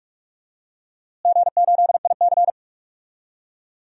Morse code, heard as a keyed tone, about 46 wpm, 700 Hz. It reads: G8IC
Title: DXCC-100 Call Signs - 4 Characters - Repeated 2x - 45wpm